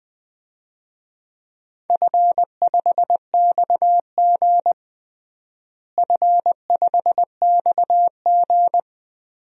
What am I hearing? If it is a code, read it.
Morse code, heard as a keyed tone, about 20 wpm, 700 Hz. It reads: F5XG F5XG
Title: DXCC-100 Call Signs - 4 Characters - Repeated 3x - 20wpm